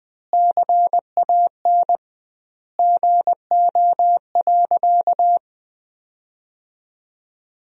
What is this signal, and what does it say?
Morse code, heard as a keyed tone, about 20 wpm, 700 Hz. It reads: CAN GO.
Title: 2x Word-Spacing - Sentences from Top 100 Words - 20wpm